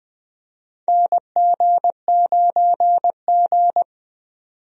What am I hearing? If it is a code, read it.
Morse code, heard as a keyed tone, about 20 wpm, 700 Hz. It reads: NG9G
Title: USA Call Signs - 4 Characters - 20wpm